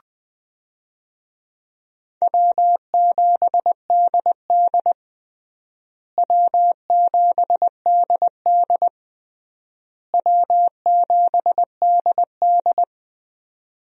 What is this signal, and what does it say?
Morse code, heard as a keyed tone, about 20 wpm, 700 Hz. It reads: W7DD W7DD W7DD
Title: USA Call Signs - 4 Characters - Repeated 3x - 20wpm